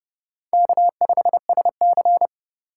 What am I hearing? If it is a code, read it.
Morse code, heard as a keyed tone, about 29 wpm, 700 Hz. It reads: K5SC
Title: USA Call Signs - 4 Characters - 4x Character Spacing - 30wpm